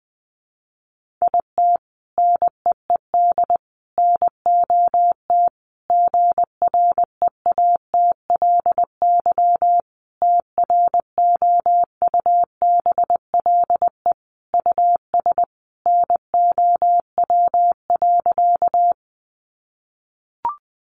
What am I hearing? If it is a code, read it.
Morse code, heard as a keyed tone, about 20 wpm, 700 Hz. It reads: IT NEED NOT GREATLY TROUBLE US NOW.
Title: Sentences from Top 1400 Words 20wpm